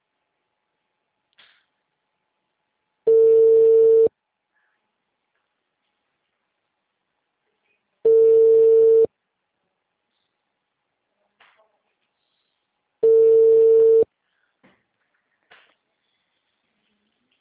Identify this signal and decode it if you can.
Ringback tone (China, 450 Hz).